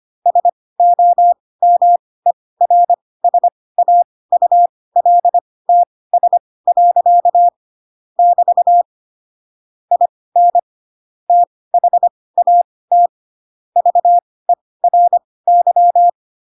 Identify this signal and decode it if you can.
Morse code, heard as a keyed tone, about 25 wpm, spaced as for 18 wpm, 700 Hz. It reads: SOMERSAULTS. <BT> IN THAT VERY
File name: Pinocchio / Pinocchio ch2 12 25 ik1hge